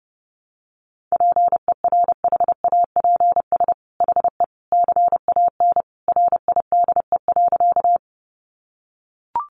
Morse code, heard as a keyed tone, about 30 wpm, 700 Hz. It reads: PERHAPS HE CAN RIDE.
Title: Sentences from Top 1200 Words 30wpm